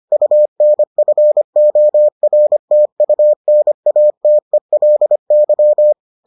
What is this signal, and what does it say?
Morse code, heard as a keyed tone, about 25 wpm, 600 Hz. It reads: UNFORTUNATELY